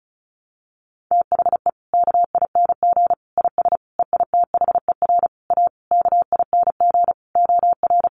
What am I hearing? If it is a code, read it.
Morse code, heard as a keyed tone, about 34 wpm, 700 Hz. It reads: THE KING IS EITHER A KING OR
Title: Sentences from Top 600 Words 35wpm